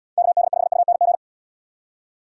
Morse code, heard as a keyed tone, about 76 wpm, 700 Hz. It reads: KD5UNG